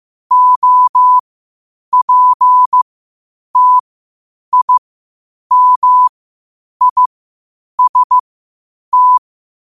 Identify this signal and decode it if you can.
Morse code, heard as a keyed tone, about 15 wpm, spaced as for 8 wpm, 1000 Hz. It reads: OPTIMIST
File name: qotdfiles / 8wpm